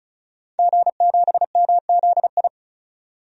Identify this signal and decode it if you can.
Morse code, heard as a keyed tone, about 34 wpm, 700 Hz. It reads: G7MZI
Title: DXCC-100 Call Signs - 5 Characters - 35wpm